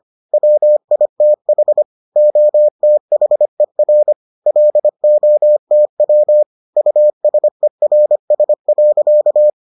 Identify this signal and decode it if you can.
Morse code, heard as a keyed tone, about 25 wpm, 600 Hz. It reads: WITH OTHER LOTW USERS.